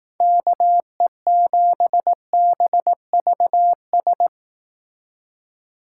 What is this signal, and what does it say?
Morse code, heard as a keyed tone, about 18 wpm, 700 Hz. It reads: KE7BVS
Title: Speed Racing - USA Call Signs 15wpm